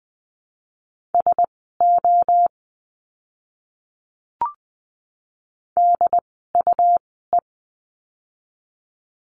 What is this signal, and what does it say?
Morse code, heard as a keyed tone, about 20 wpm, spaced as for 14 wpm, 700 Hz. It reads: SO DUE
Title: 101 Lesson 17f - Letter u - Words 20wpm